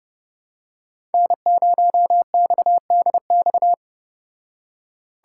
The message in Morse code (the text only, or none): N0XDX